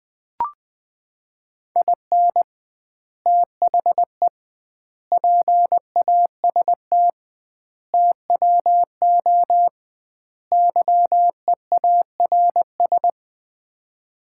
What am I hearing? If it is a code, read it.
Morse code, heard as a keyed tone, about 20 wpm, 700 Hz. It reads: IN THE PAST TWO YEARS
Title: 2x Word-Spacing - Sets of 5 Words - 20wpm